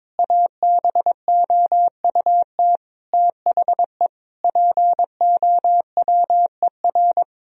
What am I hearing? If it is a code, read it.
Morse code, heard as a keyed tone, about 22 wpm, 700 Hz. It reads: ABOUT THE POWER
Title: Sentences from Top 1000 Words - Review Part 4 - 22wpm